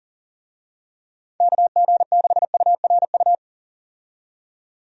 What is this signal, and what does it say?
Morse code, heard as a keyed tone, about 40 wpm, 700 Hz. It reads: KG6URU
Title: USA Call Signs - 6 Characters - 40wpm